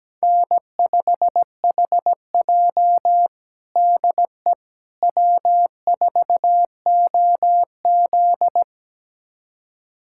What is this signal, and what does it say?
Morse code, heard as a keyed tone, about 17 wpm, 700 Hz. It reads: N5HJ DE W4OZ
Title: USA Callsign DE Callsign - 4 Characters - 17wpm